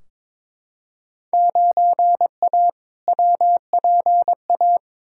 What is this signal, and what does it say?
Morse code, heard as a keyed tone, about 22 wpm, 700 Hz. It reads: 9A WPA
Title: Field Day Exchanges Encore - 22wpm